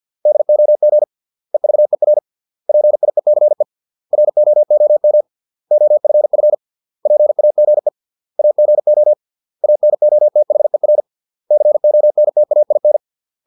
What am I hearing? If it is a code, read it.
Morse code, heard as a keyed tone, about 50 wpm, 600 Hz. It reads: DOG EVER PIECE ROOM OFF PAGE AGO ANOTHER CONTAIN